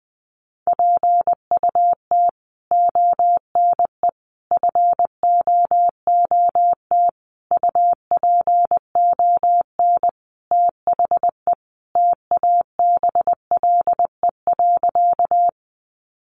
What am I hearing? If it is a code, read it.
Morse code, heard as a keyed tone, about 20 wpm, 700 Hz. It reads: PUT ONE FOOT UPON THE TABLE.
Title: Sentences from Top 1100 Words 20wpm